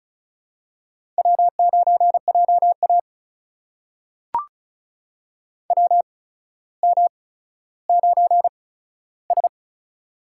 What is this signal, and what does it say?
Morse code, heard as a keyed tone, about 36 wpm, spaced as for 15 wpm, 700 Hz. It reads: W9JA WM9S
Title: USA Call Signs - 4 Characters - 8x Character Spacing - 35wpm